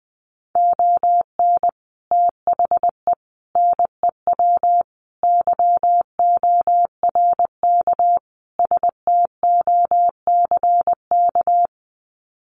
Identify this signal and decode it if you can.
Morse code, heard as a keyed tone, about 20 wpm, 700 Hz. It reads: ON THE NEW YORK STOCK
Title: Sets of 5 Words Encore 20wpm